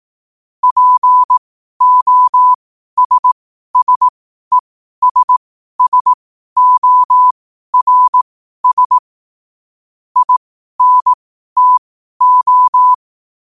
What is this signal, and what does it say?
Morse code, heard as a keyed tone, about 18 wpm, spaced as for 12 wpm, 1000 Hz. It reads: POSSESSORS INTO
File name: qotdfiles / 12wpm